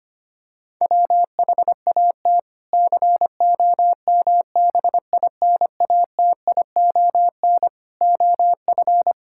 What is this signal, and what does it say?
Morse code, heard as a keyed tone, about 25 wpm, 700 Hz. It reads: WHAT COMBINATION OF
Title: Sentences from Top 1000 Words - Review Part 6 - 25wpm